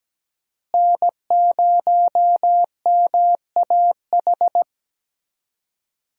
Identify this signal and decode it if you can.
Morse code, heard as a keyed tone, about 17 wpm, 700 Hz. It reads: N0MAH
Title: USA Call Signs - 5 Characters - Repeated 3x - 17wpm